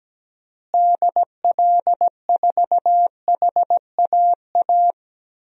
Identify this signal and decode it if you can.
Morse code, heard as a keyed tone, about 17 wpm, 700 Hz. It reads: DL4HAA
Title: DXCC-100 Call Signs - 6 Characters - Repeated 3x - 17wpm